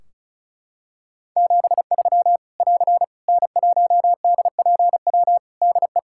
Morse code, heard as a keyed tone, about 36 wpm, 700 Hz. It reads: 73 <AR> N1DPW DE
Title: Practice QSO - Exchange 4 - Ending - 35wpm